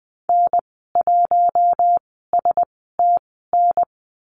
Morse code, heard as a keyed tone, about 20 wpm, spaced as for 14 wpm, 700 Hz. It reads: N1STN
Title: 062 Lesson 11g - Letter h - Call Signs 20wpm